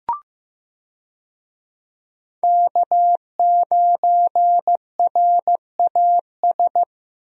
Morse code, heard as a keyed tone, about 15 wpm, 700 Hz. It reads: K9RAS
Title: USA Call Signs - 5 Characters - Repeated 2x - 15wpm